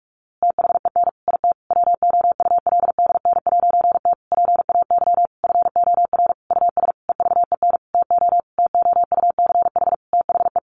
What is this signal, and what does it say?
Morse code, heard as a keyed tone, about 44 wpm, 700 Hz. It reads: THEN IT WOULDN'T PAY FOR US EVEN TO TOUCH THE
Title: A Sound Of Thunder 45wpm